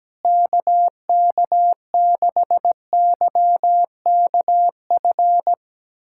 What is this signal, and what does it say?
Morse code, heard as a keyed tone, about 17 wpm, 700 Hz. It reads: KK6YKF
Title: USA Callsign DE Callsign 17wpm